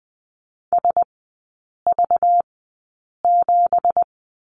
Morse code, heard as a keyed tone, about 20 wpm, spaced as for 8 wpm, 700 Hz. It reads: SV7
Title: 191 Lesson 31e - Letter v - Three Characters 20wpm